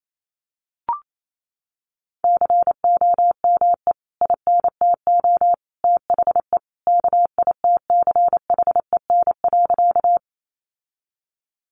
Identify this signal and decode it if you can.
Morse code, heard as a keyed tone, about 28 wpm, 700 Hz. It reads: COME INTO THE KITCHEN.
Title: Sentences from Top 1400 Words 28wpm